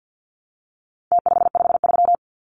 Morse code, heard as a keyed tone, about 50 wpm, 700 Hz. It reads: T553
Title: Numbers - Contest Style - 4-Digit 50wpm